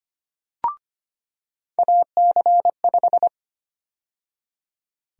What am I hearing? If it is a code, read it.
Morse code, heard as a keyed tone, about 25 wpm, 700 Hz. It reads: AC5
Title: USA Call Sign Prefixes - 25wpm